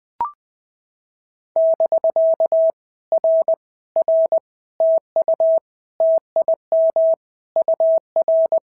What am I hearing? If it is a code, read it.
Morse code, heard as a keyed tone, about 19 wpm, 650 Hz. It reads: <BK> R R TU TIM UR
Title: POTA QSO Practice - 20wpm